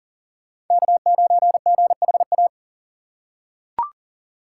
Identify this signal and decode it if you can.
Morse code, heard as a keyed tone, about 40 wpm, 700 Hz. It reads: K9GHA
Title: USA Call Signs - 3x Character Spacing - 40wpm